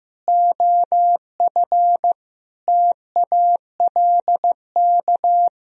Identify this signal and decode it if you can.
Morse code, heard as a keyed tone, about 16 wpm, 700 Hz. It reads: OF TALK
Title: Sentences from Top 1000 Words - Review Part 4 - 15wpm